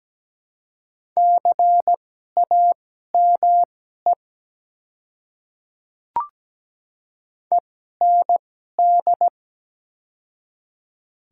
Morse code, heard as a keyed tone, about 17 wpm, spaced as for 12 wpm, 700 Hz. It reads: CAME END
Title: 2x Character Spacing - Top 200 Words - 17wpm